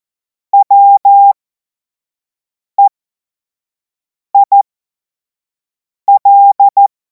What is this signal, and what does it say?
Morse code, heard as a keyed tone, about 14 wpm, spaced as for 5 wpm, 800 Hz. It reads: WEIL
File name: Morsekurs / Witze / Witz 12. Mai 2026 25 F70